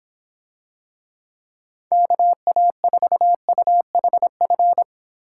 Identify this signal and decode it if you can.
Morse code, heard as a keyed tone, about 25 wpm, 700 Hz. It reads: KA4UHF